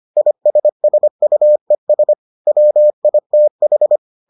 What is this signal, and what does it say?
Morse code, heard as a keyed tone, about 25 wpm, 600 Hz. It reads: ISSUES WITH